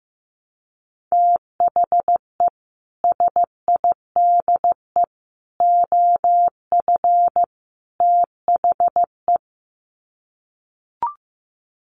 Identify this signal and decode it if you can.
Morse code, heard as a keyed tone, about 15 wpm, 700 Hz. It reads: THE SIDE OF THE
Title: Sets of 4 Words 15wpm